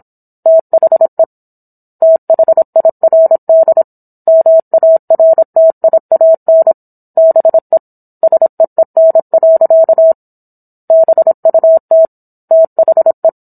Morse code, heard as a keyed tone, about 27 wpm, 650 Hz. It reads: THE THIRD MARTIAN BE SEEN. BUT THE